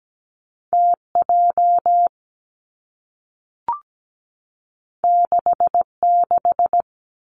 Morse code, heard as a keyed tone, about 17 wpm, 700 Hz. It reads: TJ 66
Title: Two Challenging Letter-Number Combinations 17wpm